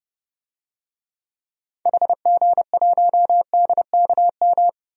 Morse code, heard as a keyed tone, about 30 wpm, 700 Hz. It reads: HG1DKM
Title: DXCC-100 Call Signs - 6 Characters - Repeated 3x - 30wpm